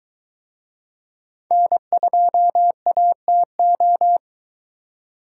Morse code, heard as a keyed tone, about 23 wpm, 700 Hz. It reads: N2ATO